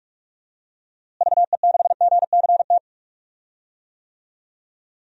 Morse code, heard as a keyed tone, about 46 wpm, 700 Hz. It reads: VE6GCT
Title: DXCC-100 Call Signs - 6 Characters - Repeated 2x - 45wpm